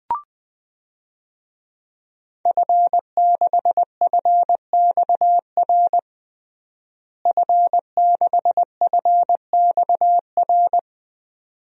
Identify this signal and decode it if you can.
Morse code, heard as a keyed tone, about 21 wpm, 700 Hz. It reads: F6FXR F6FXR
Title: DXCC-100 Call Signs - 5 Characters - Repeated 2x - 20wpm